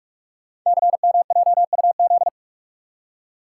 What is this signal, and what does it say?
Morse code, heard as a keyed tone, about 44 wpm, 700 Hz. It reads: CMJUZ
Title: Code Groups - 5 Letters - 45wpm